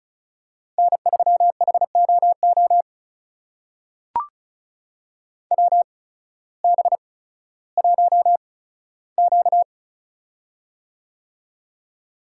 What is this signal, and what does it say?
Morse code, heard as a keyed tone, about 36 wpm, spaced as for 16 wpm, 700 Hz. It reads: N3HOO WB1Q
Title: USA Call Signs - 8x Character Spacing - 35wpm